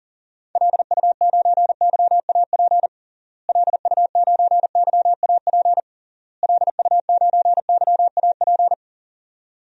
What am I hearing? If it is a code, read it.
Morse code, heard as a keyed tone, about 40 wpm, 700 Hz. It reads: LU9YAP LU9YAP LU9YAP
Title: DXCC-100 Call Signs - 6 Characters - Repeated 3x - 40wpm